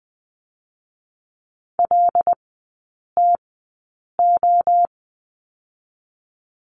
Morse code, heard as a keyed tone, about 20 wpm, 700 Hz. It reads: L T O
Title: 100 Lesson 17e - Letter u - Three Characters 20wpm